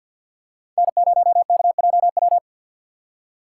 Morse code, heard as a keyed tone, about 48 wpm, 700 Hz. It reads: N0KJW
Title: USA Call Signs - 5 Characters - 3x Character Spacing - 50wpm